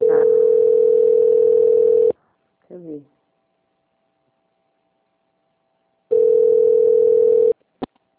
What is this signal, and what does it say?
Ringback tone (North America), 440 + 480 Hz, with speech.
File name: out- - - - - .3275